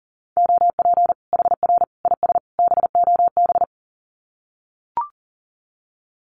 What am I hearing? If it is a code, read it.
Morse code, heard as a keyed tone, about 40 wpm, 700 Hz. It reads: OP HR IS BOB
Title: Common QSO Phrases 40wpm